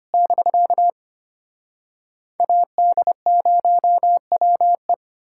Morse code, heard as a keyed tone, about 33 wpm, 700 Hz. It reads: <BK> AD0WE